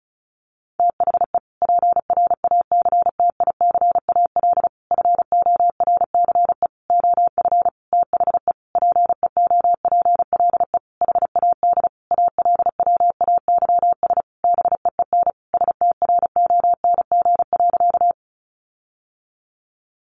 Morse code, heard as a keyed tone, about 34 wpm, 700 Hz. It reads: THE PRACTICAL FORCE OF THE PEOPLE HAD ALWAYS BEEN STRONG.
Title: Sentences from Top 1400 Words 35wpm